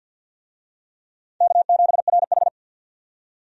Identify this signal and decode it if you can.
Morse code, heard as a keyed tone, about 53 wpm, 700 Hz. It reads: K7RH
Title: USA Call Signs - 4 Characters - 50wpm